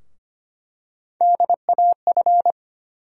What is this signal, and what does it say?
Morse code, heard as a keyed tone, about 25 wpm, 700 Hz. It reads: DAF
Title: USA Call Sign Postfixes - 3 Letters 25wpm